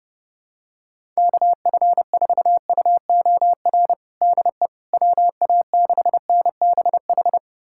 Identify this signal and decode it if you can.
Morse code, heard as a keyed tone, about 32 wpm, 700 Hz. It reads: KF4UOR DE WA6NBH
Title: USA Callsign DE Callsign - 6 Characters - 30wpm